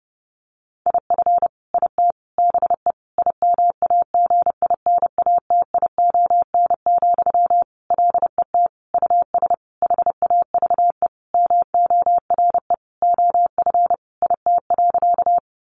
Morse code, heard as a keyed tone, about 30 wpm, 700 Hz. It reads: IF IT BE IMAGINATION, LET US HAVE MORE OF IT.
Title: Sentences from Top 1300 Words 30wpm